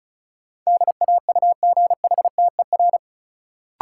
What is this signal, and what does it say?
Morse code, heard as a keyed tone, about 36 wpm, 700 Hz. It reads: DAUGHTER
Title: Sentences from Top 1500 Words - New Word Drill Alt - 35wpm